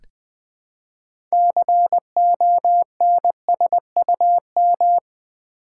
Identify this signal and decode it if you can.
Morse code, heard as a keyed tone, about 20 wpm, 700 Hz. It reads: CONSUM
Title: Build A Word - 3 Letters At A Time - 8-Letter Words - 20wpm